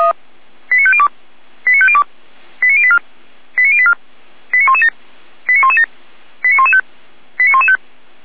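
Five-tone selective call (ZVEI), heard as quick runs of five tones, tones of about 70 ms. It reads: 89502 89502 89084 89084 89117 89117 89115 89115
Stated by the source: ZVEI five-tone call received by SDR.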